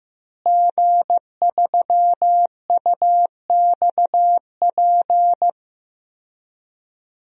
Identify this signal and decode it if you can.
Morse code, heard as a keyed tone, about 15 wpm, 700 Hz. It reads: G3UXP
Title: DXCC-100 Call Signs - 5 Characters - Repeated 2x - 15wpm